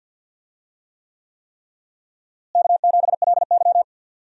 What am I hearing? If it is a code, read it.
Morse code, heard as a keyed tone, about 52 wpm, 700 Hz. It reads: K7LY
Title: USA Call Signs - 4 Characters - Repeated 3x - 50wpm